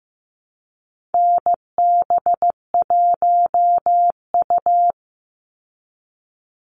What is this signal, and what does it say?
Morse code, heard as a keyed tone, about 15 wpm, 700 Hz. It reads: NB1U
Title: Call Signs Repeated 2x - 15wpm